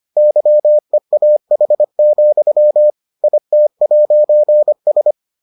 Morse code, heard as a keyed tone, about 25 wpm, 600 Hz. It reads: YEAH, IT'S